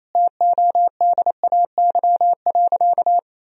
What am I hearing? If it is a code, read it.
Morse code, heard as a keyed tone, about 28 wpm, 700 Hz. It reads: TODAY.
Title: Sentences from Top 1000 Words - Review Part 3 - 28wpm